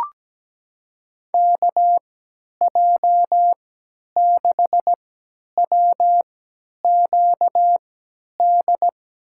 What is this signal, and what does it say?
Morse code, heard as a keyed tone, about 17 wpm, spaced as for 10 wpm, 700 Hz. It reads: KJ6WQD
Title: USA Call Signs - 3x Character Spacing - 17wpm